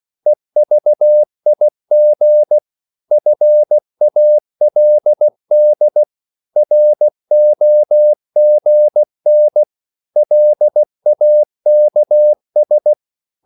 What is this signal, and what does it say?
Morse code, heard as a keyed tone, about 16 wpm, 600 Hz. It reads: EVIG FALD ROGN LAKS
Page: Korte danske ord 16wpm